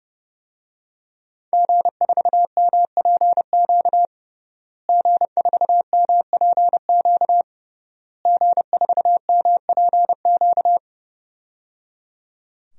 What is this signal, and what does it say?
Morse code, heard as a keyed tone, about 30 wpm, 700 Hz. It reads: G4MPQ G4MPQ G4MPQ
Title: DXCC-100 Call Signs - 5 Characters - Repeated 3x - 30wpm